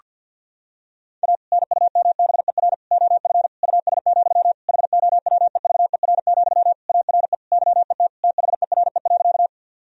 Morse code, heard as a keyed tone, about 53 wpm, 700 Hz. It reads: A NUMBER OF US, HOWEVER, ARE YET THERE.
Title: Sentences from Top 1000 Words - Review Part 1 - 50wpm